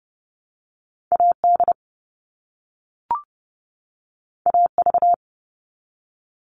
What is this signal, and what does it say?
Morse code, heard as a keyed tone, about 30 wpm, 700 Hz. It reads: AD AV
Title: All Cut Numbers - 2-Digit 30wpm